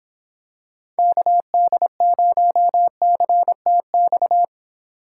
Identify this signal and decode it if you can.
Morse code, heard as a keyed tone, about 26 wpm, 700 Hz. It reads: KD0CTX